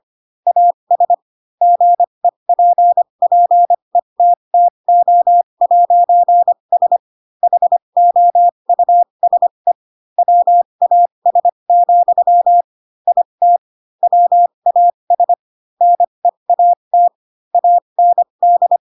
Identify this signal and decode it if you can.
Morse code, heard as a keyed tone, about 25 wpm, spaced as for 22 wpm, 700 Hz. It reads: AS GEPPETTO'S HOUSE WAS, IT WAS NEAT AND
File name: Pinocchio / Pinocchio ch3 18 25 ik1hge